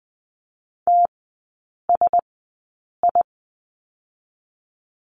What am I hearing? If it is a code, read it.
Morse code, heard as a keyed tone, about 20 wpm, spaced as for 8 wpm, 700 Hz. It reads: TSI
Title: 035 Lesson 7e - Letter s - Three Characters 20wpm